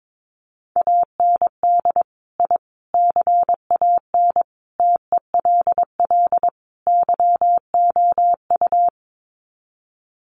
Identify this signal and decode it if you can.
Morse code, heard as a keyed tone, about 22 wpm, 700 Hz. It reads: AND I CAN TELL YOU
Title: Sets of 5 Words Encore2 22wpm